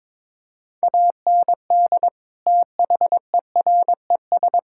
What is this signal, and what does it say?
Morse code, heard as a keyed tone, about 22 wpm, 700 Hz. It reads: AND THERES
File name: practice / 22wpm / Sets of 8 Words - 22wpm